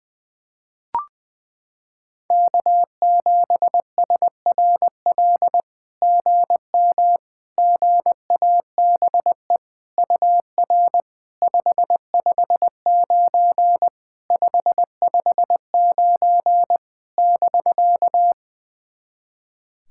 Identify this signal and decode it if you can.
Morse code, heard as a keyed tone, about 21 wpm, 700 Hz. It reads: K7SRL GM GABE UR 559 559 <BK>
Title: SOTA QSO Practice 20wpm